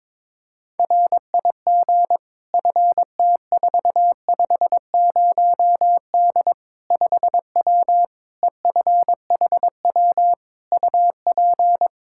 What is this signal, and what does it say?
Morse code, heard as a keyed tone, about 22 wpm, 700 Hz. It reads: RIG FT450D 5W EFHW UP
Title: Practice QSO - Exchange 2 - Info - 22wpm